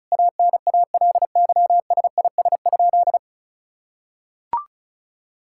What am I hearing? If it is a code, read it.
Morse code, heard as a keyed tone, about 36 wpm, 700 Hz. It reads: ANALYSIS?
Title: Sentences from Top 1000 Words - Review Part 5 - 35wpm